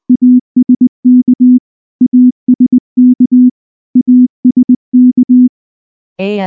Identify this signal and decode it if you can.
Morse code, heard as a keyed tone, about 19 wpm, 250 Hz. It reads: ASK ASK ASK
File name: morse / morse-training-20WPM-8